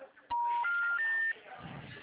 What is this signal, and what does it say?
Special information tone, with noise.